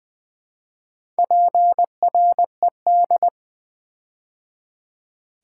Morse code, heard as a keyed tone, about 20 wpm, 700 Hz. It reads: PRED
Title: Build A Word - 4 Letters At A Time - 11-Letter Words - 20wpm